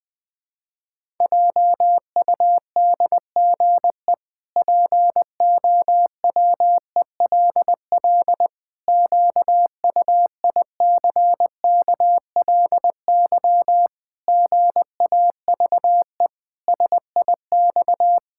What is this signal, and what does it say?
Morse code, heard as a keyed tone, about 20 wpm, 700 Hz. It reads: JUDGE POWELL QUICKLY GAVE SIX
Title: Pangram Challenge v2 - 20wpm